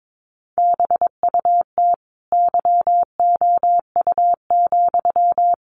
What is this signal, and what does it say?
Morse code, heard as a keyed tone, about 22 wpm, 700 Hz. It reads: BUT YOU,
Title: A Sound Of Thunder 22wpm